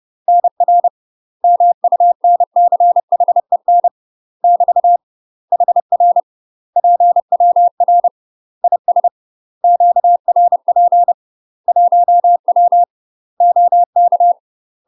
Morse code, heard as a keyed tone, about 30 wpm, 700 Hz. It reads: NR MUNCHEN <BT> HR PWR IS QRP 1W OK
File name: UK MASTER QSOs TRAINING / 30WPM / UK MASTER QSOs 0009